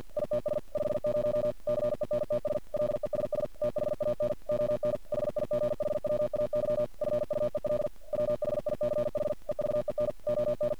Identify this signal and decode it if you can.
Morse code, heard as a keyed tone, about 50 wpm, 600 Hz. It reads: ITS 50 CENTS LESS THAN ON HIGHWAY FUEL WHICH EVEN ON
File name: archived / CRSnet